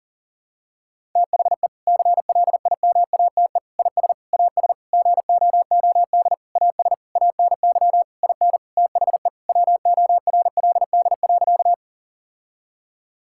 Morse code, heard as a keyed tone, about 40 wpm, 700 Hz. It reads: THE CLIMATE IS AS GOOD AS ANY IN THE WORLD.
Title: Sentences from Top 1600 Words 40wpm